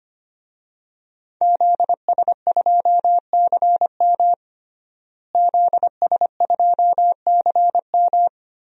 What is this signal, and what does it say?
Morse code, heard as a keyed tone, about 25 wpm, 700 Hz. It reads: ZS2CM ZS2CM
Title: DXCC-100 Call Signs - 5 Characters - Repeated 2x - 25wpm